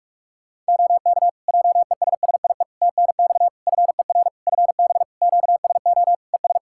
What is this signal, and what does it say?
Morse code, heard as a keyed tone, about 46 wpm, 700 Hz. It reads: OK JESSIE TNX FER FB QSO ES
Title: Practice QSO - Exchange 4 - Ending - 45wpm